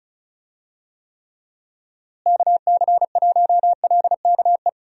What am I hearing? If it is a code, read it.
Morse code, heard as a keyed tone, about 36 wpm, 700 Hz. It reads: KC1LKE
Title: USA Call Signs - 6 Characters - Repeated 3x - 35wpm